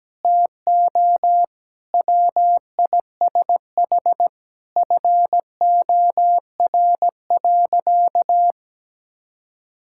Morse code, heard as a keyed tone, about 17 wpm, 700 Hz. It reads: TO WISH FOR.
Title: Sentences from Top 1600 Words 17wpm